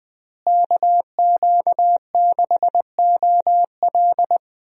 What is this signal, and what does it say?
Morse code, heard as a keyed tone, about 21 wpm, 700 Hz. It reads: KQ6OL
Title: USA Call Signs - 5 Characters - Repeated 3x - 20wpm